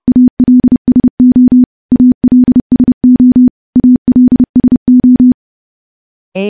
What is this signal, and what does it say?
Morse code, heard as a keyed tone, about 32 wpm, 250 Hz. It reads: ALSO ALSO ALSO
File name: morse / morse-training-30WPM